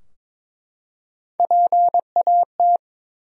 Morse code, heard as a keyed tone, about 23 wpm, 700 Hz. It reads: PAT
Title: USA Common Names - Max 5 Letters - 22wpm